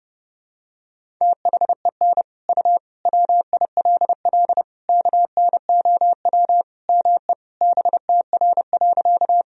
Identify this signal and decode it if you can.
Morse code, heard as a keyed tone, about 30 wpm, 700 Hz. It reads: THEN U WILL KNOW ME BTR.
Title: Sentences from Top 400 Words and Abbreviations 30wpm